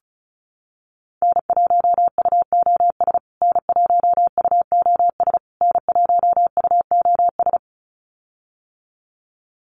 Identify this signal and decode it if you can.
Morse code, heard as a keyed tone, about 34 wpm, 700 Hz. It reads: N1UOS N1UOS N1UOS
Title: USA Call Signs Repeated 3x 35wpm